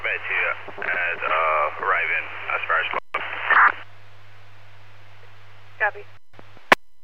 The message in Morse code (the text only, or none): none